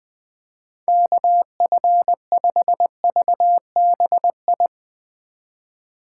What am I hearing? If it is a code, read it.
Morse code, heard as a keyed tone, about 20 wpm, 700 Hz. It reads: KF5VBI
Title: Speed Racing - USA 2x3 Call Signs 17wpm